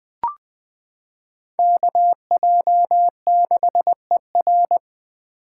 Morse code, heard as a keyed tone, about 20 wpm, 700 Hz. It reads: KJ6ER
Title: 2023 POTA Activator Call Signs - 20wpm